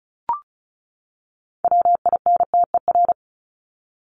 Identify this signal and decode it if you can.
Morse code, heard as a keyed tone, about 34 wpm, 700 Hz. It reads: WINTER
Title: Top 1501-1750 Words 35wpm